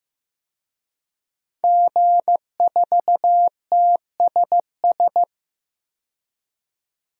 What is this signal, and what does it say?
Morse code, heard as a keyed tone, about 15 wpm, 700 Hz. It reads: G4TSS
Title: DXCC-100 Call Signs - 5 Characters - 15wpm